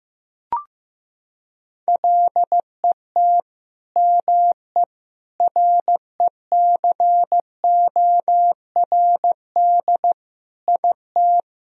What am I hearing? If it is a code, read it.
Morse code, heard as a keyed tone, about 15 wpm, 700 Hz. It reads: LET ME RECORD IT
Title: Sentences from Top 1000 Words - Review Part 3 - 15wpm